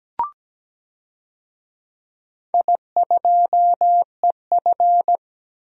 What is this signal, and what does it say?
Morse code, heard as a keyed tone, about 17 wpm, 700 Hz. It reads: I2EF
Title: DXCC-100 Call Signs - 4 Characters - Repeated 3x - 17wpm